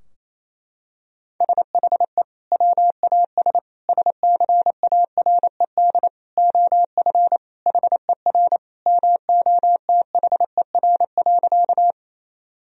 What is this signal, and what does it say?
Morse code, heard as a keyed tone, about 28 wpm, 700 Hz. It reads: SHE WAS SCARED OF HER MOTHER.
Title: Sentences from Top 1000 Words - Review Part 4 - 28wpm